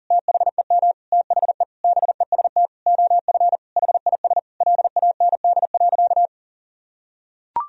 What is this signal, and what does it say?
Morse code, heard as a keyed tone, about 40 wpm, 700 Hz. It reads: THEM THE BEST OF HIS LAND.
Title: Sentences from Top 1000 Words - Review Part 1 - 40wpm